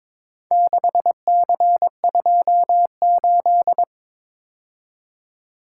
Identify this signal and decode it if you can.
Morse code, heard as a keyed tone, about 22 wpm, 700 Hz. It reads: 6C28
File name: practice / 22wpm / Four Letter-Number Combinations - 22wpm